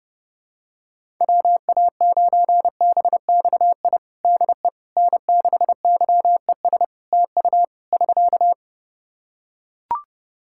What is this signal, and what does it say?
Morse code, heard as a keyed tone, about 32 wpm, 700 Hz. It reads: WA9BXI DE N6YES TU <SK>
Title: Practice QSO - Exchange 4 - Ending - 30wpm